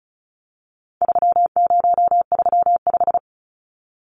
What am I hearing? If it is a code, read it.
Morse code, heard as a keyed tone, about 34 wpm, 700 Hz. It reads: 3035
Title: Numbers - 4-Digit 35wpm